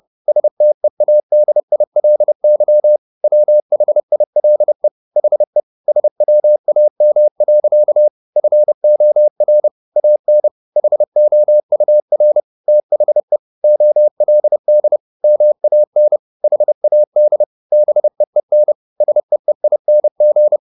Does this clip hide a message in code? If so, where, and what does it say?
Yes: Morse code from 0.3 to 20.6 s, heard as a keyed tone, about 30 wpm, 600 Hz. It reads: STEADILY WHILE HE SWAM. FOR AN HOUR THE OLD MAN HAD BEEN SEEING